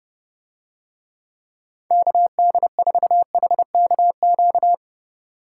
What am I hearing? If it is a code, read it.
Morse code, heard as a keyed tone, about 29 wpm, 700 Hz. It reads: KD4HKQ